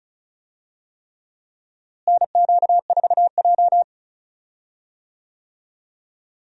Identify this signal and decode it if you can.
Morse code, heard as a keyed tone, about 36 wpm, 700 Hz. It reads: NQ4J
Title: USA Call Signs - 4 Characters - Repeated 3x - 35wpm